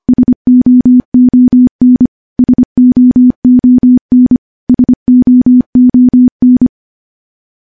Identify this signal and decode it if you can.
Morse code, heard as a keyed tone, about 25 wpm, 250 Hz. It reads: SOON SOON SOON
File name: morse / morse-training-25WPM